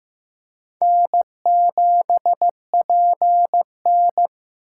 Morse code, heard as a keyed tone, about 15 wpm, 700 Hz. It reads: N7PN
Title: USA Call Signs - 6x Character Spacing - 15wpm